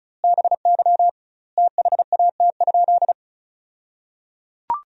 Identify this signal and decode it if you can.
Morse code, heard as a keyed tone, about 36 wpm, 700 Hz. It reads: BY THAT?
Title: 2x Word-Spacing - Sentences from Top 200 Words - 35wpm